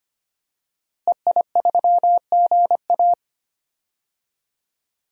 Morse code, heard as a keyed tone, about 25 wpm, 700 Hz. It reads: EI3GA
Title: DXCC-100 Call Signs - 5 Characters - Repeated 2x - 25wpm